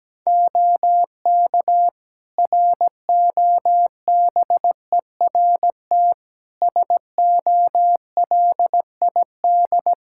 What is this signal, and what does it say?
Morse code, heard as a keyed tone, about 17 wpm, 700 Hz. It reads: OK ROBERT SOLID
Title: K1USN SST QSO Practice 17wpm